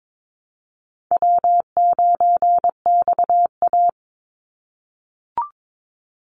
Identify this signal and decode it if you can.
Morse code, heard as a keyed tone, about 22 wpm, 700 Hz. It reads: W9XA
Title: Easy Call Signs - 22wpm